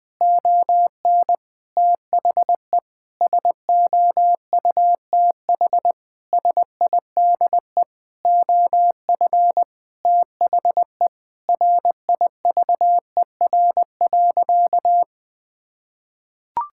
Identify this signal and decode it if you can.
Morse code, heard as a keyed tone, about 20 wpm, 700 Hz. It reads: ON THE SOUTH SIDE OF THE RIVER.
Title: Sentences from Top 1000 Words - Review Part 5 - 20wpm